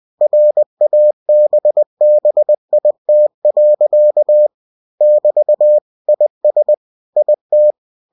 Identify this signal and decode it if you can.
Morse code, heard as a keyed tone, about 20 wpm, 600 Hz. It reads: RABBIT. <BT> IS IT
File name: cw-books / winnie-the-pooh / winnie-20wpm-page-0069